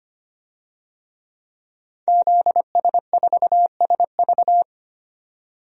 Morse code, heard as a keyed tone, about 25 wpm, 700 Hz. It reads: ZS4SV
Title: DXCC-100 Call Signs - 5 Characters - 25wpm